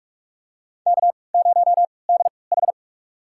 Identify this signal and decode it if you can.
Morse code, heard as a keyed tone, about 46 wpm, spaced as for 26 wpm, 700 Hz. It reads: K0DH
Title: USA Call Signs - 4 Characters - 3x Character Spacing - 45wpm